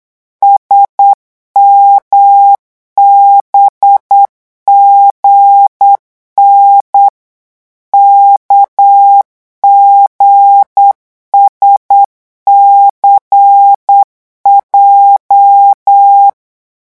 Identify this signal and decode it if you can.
Morse code, heard as a keyed tone, about 8 wpm, 800 Hz. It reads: SMBGN KGSCJ
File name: morse mp3 / revision8-vitesse 8 mots